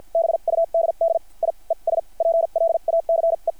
Morse code, heard as a keyed tone, about 50 wpm, spaced as for 38 wpm, 660 Hz. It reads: BUDDIES PLACE
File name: archived / CRSnet